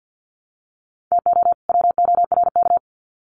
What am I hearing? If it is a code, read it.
Morse code, heard as a keyed tone, about 50 wpm, 700 Hz. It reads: TO WORK
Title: Sets of 2 Words 50wpm